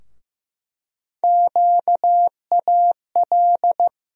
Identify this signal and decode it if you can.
Morse code, heard as a keyed tone, about 16 wpm, 700 Hz. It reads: QAL
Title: USA Call Sign Postfixes - 3 Letters 15wpm